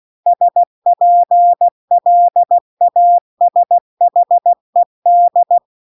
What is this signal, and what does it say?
Morse code, heard as a keyed tone, about 16 wpm, 700 Hz. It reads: SPLASHED
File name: CW-The-Big-Twoheaded-River-2-16WPM